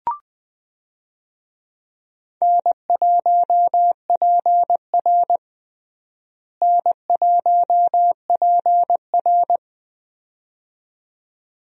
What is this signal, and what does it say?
Morse code, heard as a keyed tone, about 20 wpm, 700 Hz. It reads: N1PR N1PR
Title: USA Call Signs - 4 Characters - Repeated 2x - 20wpm